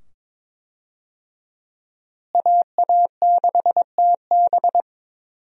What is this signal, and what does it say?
Morse code, heard as a keyed tone, about 22 wpm, 700 Hz. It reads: AA6TB
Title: USA Call Signs - 5 Characters - Repeated 2x - 22wpm